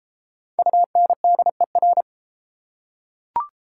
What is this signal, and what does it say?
Morse code, heard as a keyed tone, about 31 wpm, 700 Hz. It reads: UNDER